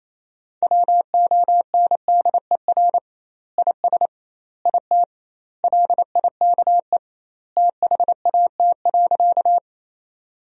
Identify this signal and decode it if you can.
Morse code, heard as a keyed tone, about 28 wpm, 700 Hz. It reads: WONDER IS IT LIKE THAT.
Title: 2x Word-Spacing - Sentences from Top 500 Words - 28wpm